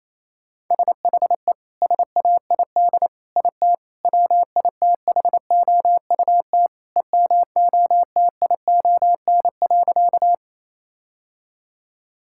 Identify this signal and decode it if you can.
Morse code, heard as a keyed tone, about 28 wpm, 700 Hz. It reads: SHE SAID IT WITHOUT EMOTION.
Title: Sentences from Top 1600 Words 28wpm